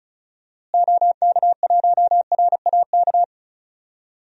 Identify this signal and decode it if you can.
Morse code, heard as a keyed tone, about 36 wpm, 700 Hz. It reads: OK1RAK
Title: DXCC-100 Call Signs - 6 Characters - Repeated 3x - 35wpm